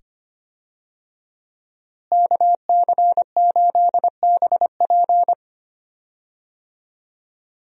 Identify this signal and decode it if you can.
Morse code, heard as a keyed tone, about 25 wpm, 700 Hz. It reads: KC8BP
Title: USA Call Signs - 5 Characters - Repeated 2x - 25wpm